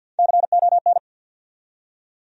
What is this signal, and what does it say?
Morse code, heard as a keyed tone, about 47 wpm, 700 Hz. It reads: CON
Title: USA Call Sign Postfixes - 3 Letters 50wpm